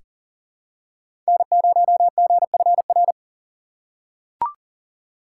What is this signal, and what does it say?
Morse code, heard as a keyed tone, about 39 wpm, 700 Hz. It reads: N0GFR
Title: USA Call Signs - 5 Characters - 2x Character Spacing - 40wpm